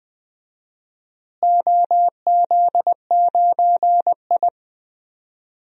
Morse code, heard as a keyed tone, about 20 wpm, 700 Hz. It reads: OZ9I
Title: DXCC-100 Call Signs - 4 Characters - Repeated 3x - 20wpm